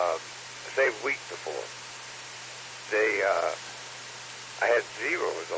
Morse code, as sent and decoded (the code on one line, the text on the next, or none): none